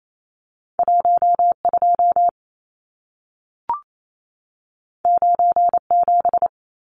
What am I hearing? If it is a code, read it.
Morse code, heard as a keyed tone, about 28 wpm, 700 Hz. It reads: 12 97
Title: Numbers - 2-Digit 28wpm